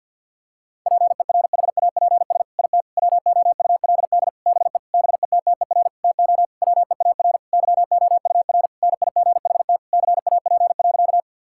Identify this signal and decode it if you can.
Morse code, heard as a keyed tone, about 50 wpm, 700 Hz. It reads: PERHAPS IT WOULD BE BETTER TO WEAR YOUR NIGHT CAP.
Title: Sentences from Top 1600 Words 50wpm